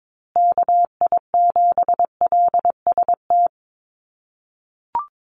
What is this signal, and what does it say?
Morse code, heard as a keyed tone, about 22 wpm, 700 Hz. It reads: KI7LST
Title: USA Call Signs Repeated 2x 22wpm